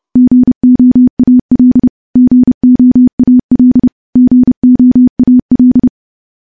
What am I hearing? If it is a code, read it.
Morse code, heard as a keyed tone, about 30 wpm, 250 Hz. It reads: GOAL GOAL GOAL
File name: morse / morse-training-30WPM